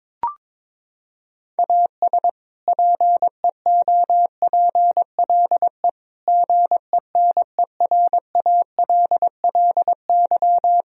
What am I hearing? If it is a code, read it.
Morse code, heard as a keyed tone, about 21 wpm, 700 Hz. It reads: AS PEOPLE GENERALLY
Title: Sentences from Top 1000 Words - Review Part 4 - 22wpm